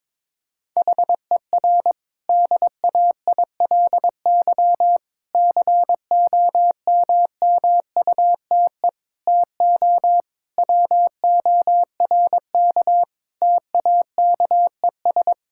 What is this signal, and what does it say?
Morse code, heard as a keyed tone, about 22 wpm, 700 Hz. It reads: HER DAILY COMMUTE TO WORK TAKES
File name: Difficult Sentences with Hints - 22wpm